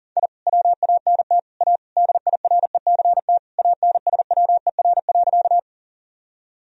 Morse code, heard as a keyed tone, about 40 wpm, 700 Hz. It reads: I WANT A DIRECT ANSWER.
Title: Sentences from Top 1000 Words - Review Part 2 - 40wpm